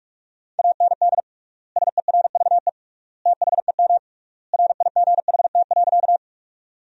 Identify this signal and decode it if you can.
Morse code, heard as a keyed tone, about 41 wpm, 700 Hz. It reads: AND SERVE THEM RIGHT.
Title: 3x Word-Spacing - Sentences from Top 400 Words - 45wpm